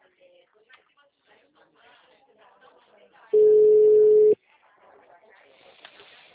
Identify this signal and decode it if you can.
Ringback tone (Europe, 425 Hz), with speech.